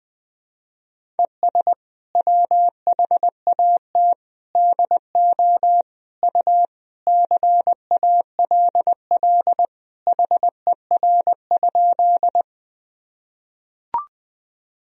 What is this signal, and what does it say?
Morse code, heard as a keyed tone, about 21 wpm, 700 Hz. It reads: ES WHAT DO U CALL HER?
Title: Sentences from Top 100 Words and Abbreviations 20wpm